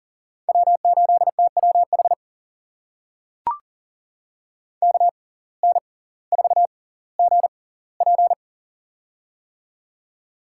Morse code, heard as a keyed tone, about 40 wpm, spaced as for 20 wpm, 700 Hz. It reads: W8TWH KN4GP
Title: USA Call Signs - 5 Characters - 6x Character Spacing - 40wpm